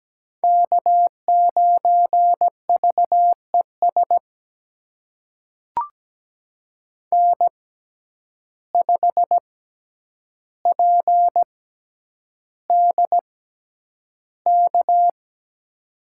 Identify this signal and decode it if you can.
Morse code, heard as a keyed tone, about 17 wpm, spaced as for 9 wpm, 700 Hz. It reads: K9VES N5PDK
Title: USA Call Signs - 5 Characters - 6x Character Spacing - 17wpm